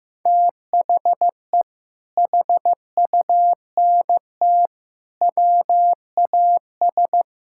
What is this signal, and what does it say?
Morse code, heard as a keyed tone, about 15 wpm, 700 Hz. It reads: THE HUNT WAS
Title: Sentences from Top 1600 Words 15wpm